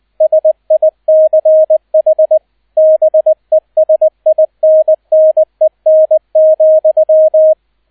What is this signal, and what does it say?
Morse code, heard as a keyed tone, about 20 wpm, 630 Hz, with radio noise. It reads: SICH BESINNEN,
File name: Erster Streich 20wpm